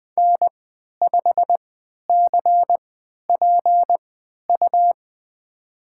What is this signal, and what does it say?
Morse code, heard as a keyed tone, about 20 wpm, spaced as for 11 wpm, 700 Hz. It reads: N5CPU
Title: USA Call Signs - 5 Characters - 3x Character Spacing - 20wpm